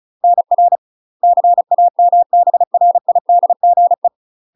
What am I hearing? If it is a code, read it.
Morse code, heard as a keyed tone, about 36 wpm, 700 Hz. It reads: NR CAMBRIDGE
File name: UK MASTER QSOs TRAINING / 35WPM / UK MASTER QSOs 0013